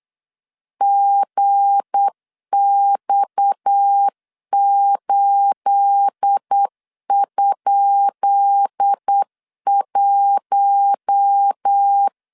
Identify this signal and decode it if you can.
Morse code, heard as a keyed tone, about 8 wpm, 800 Hz. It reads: GX8?1
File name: LEZIONI / 10 SIGNS / 10B INT 8wpm 3534